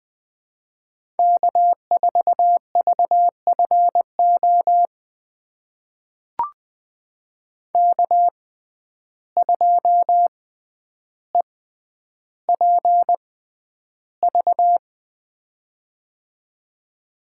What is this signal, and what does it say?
Morse code, heard as a keyed tone, about 20 wpm, spaced as for 10 wpm, 700 Hz. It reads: K4VFO K2EPV
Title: USA Call Signs - 5 Characters - 6x Character Spacing - 20wpm